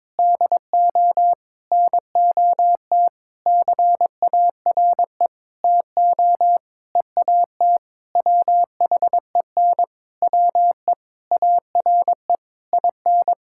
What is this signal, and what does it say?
Morse code, heard as a keyed tone, about 21 wpm, 700 Hz. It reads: DO NOT CARE TO EAT WHEN WE ARE IN
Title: Sentences from Top 1600 Words 22wpm